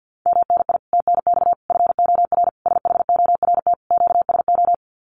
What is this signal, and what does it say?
Morse code, heard as a keyed tone, about 50 wpm, 700 Hz. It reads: MNI TNX FOR SHORT QSO
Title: Common QSO Phrases 50wpm